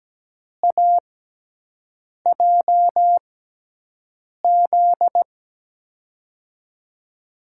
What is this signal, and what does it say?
Morse code, heard as a keyed tone, about 17 wpm, spaced as for 6 wpm, 700 Hz. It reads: AJZ
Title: USA Call Signs - 6 Characters - 6x Character Spacing - 17wpm